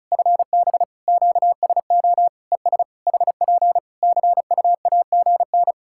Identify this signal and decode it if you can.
Morse code, heard as a keyed tone, about 34 wpm, 700 Hz. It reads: FB QSO ES HP CUAGN
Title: Practice QSO - Exchange 4 - Ending - 35wpm